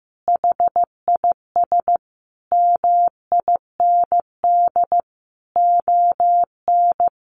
Morse code, heard as a keyed tone, about 15 wpm, 700 Hz. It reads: HIS MIND ON
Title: Sentences from Top 1400 Words 15wpm